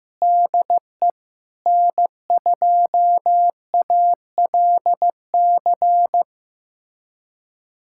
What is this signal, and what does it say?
Morse code, heard as a keyed tone, about 16 wpm, 700 Hz. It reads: DE N2ALC
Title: USA Callsign DE Callsign - 5 Characters - 15wpm